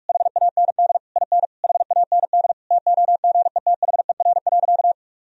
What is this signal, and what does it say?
Morse code, heard as a keyed tone, about 41 wpm, 700 Hz. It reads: HAND IN HAND TOGETHER.
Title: Sentences from Top 1000 Words - Review Part 4 - 45wpm